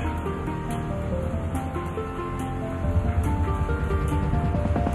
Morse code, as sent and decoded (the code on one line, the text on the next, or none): none